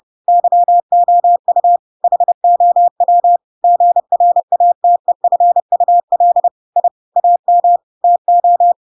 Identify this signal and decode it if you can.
Morse code, heard as a keyed tone, about 30 wpm, 700 Hz. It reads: YOU HOW GRATEFUL I AM TO
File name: Pinocchio / Pinocchio ch8 30 30 ik1hge